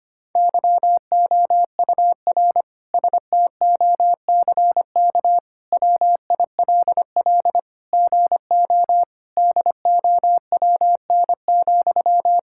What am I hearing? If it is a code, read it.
Morse code, heard as a keyed tone, about 25 wpm, 700 Hz. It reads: YOUR STOCK WILL GO DOWN,
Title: Sentences from Top 1000 Words - Review Part 5 - 25wpm